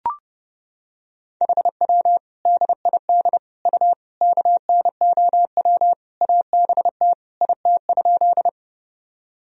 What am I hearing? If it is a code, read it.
Morse code, heard as a keyed tone, about 30 wpm, 700 Hz. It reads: HW DID U KNOW ABT IT?
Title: Sentences from Top 100 Words and Abbreviations 30wpm